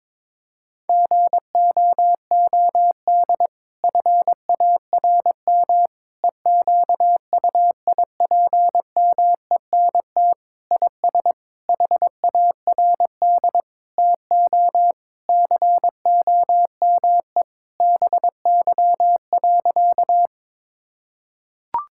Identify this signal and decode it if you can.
Morse code, heard as a keyed tone, about 22 wpm, 700 Hz. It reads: GOOD FARM EQUIPMENT IS HARD TO COME BY.
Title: Sentences from Top 1000 Words - Review Part 5 - 22wpm